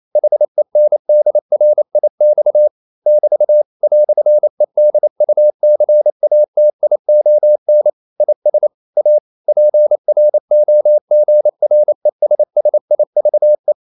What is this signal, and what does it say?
Morse code, heard as a keyed tone, about 28 wpm, 600 Hz. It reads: HENDRIX <BT> "EDUCATION IS A PROGRESSIVE
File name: CW / Practice / EnglishQuotations0228